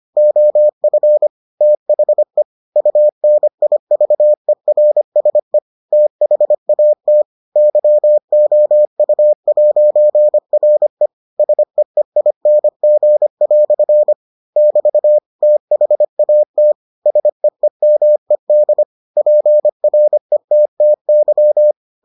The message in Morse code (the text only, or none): OF THE UNIVERSE THAT YOU'RE SEEING" <BT> THAT SEEMED PRETTY